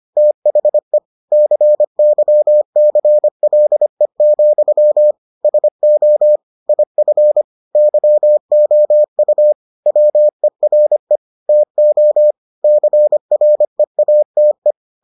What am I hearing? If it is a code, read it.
Morse code, heard as a keyed tone, about 25 wpm, 600 Hz. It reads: THE CYCLE, SO IF YOU WERE TO CREATE